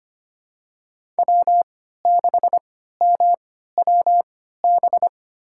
Morse code, heard as a keyed tone, about 25 wpm, spaced as for 14 wpm, 700 Hz. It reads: W6MWB
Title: USA Call Signs - 5 Characters - 3x Character Spacing - 25wpm